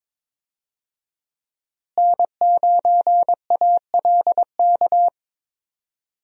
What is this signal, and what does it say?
Morse code, heard as a keyed tone, about 22 wpm, 700 Hz. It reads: N9ALK
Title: USA Call Signs - 5 Characters - Repeated 3x - 22wpm